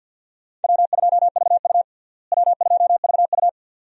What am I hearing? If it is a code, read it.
Morse code, heard as a keyed tone, about 47 wpm, 700 Hz. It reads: W2VU W2VU
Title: USA Call Signs - 4 Characters - Repeated 2x - 50wpm